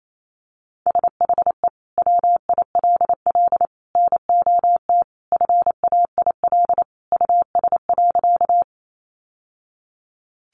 Morse code, heard as a keyed tone, about 28 wpm, 700 Hz. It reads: SHE WILL NOT FAIL US.
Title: Sentences from Top 1100 Words 28wpm